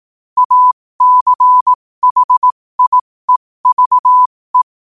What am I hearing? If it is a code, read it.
Morse code, heard as a keyed tone, about 18 wpm, spaced as for 15 wpm, 1000 Hz. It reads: ACHIEVE
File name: qotdfiles / 15wpm